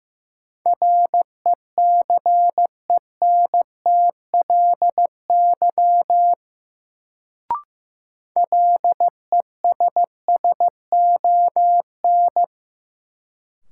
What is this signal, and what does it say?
Morse code, heard as a keyed tone, about 15 wpm, 700 Hz. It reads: RECENTLY LESSON
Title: Sentences from Top 1200 Words - New Word Drill - 15wpm